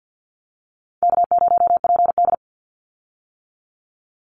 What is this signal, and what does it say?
Morse code, heard as a keyed tone, about 50 wpm, 700 Hz. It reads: K0PD
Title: Call Signs - 50wpm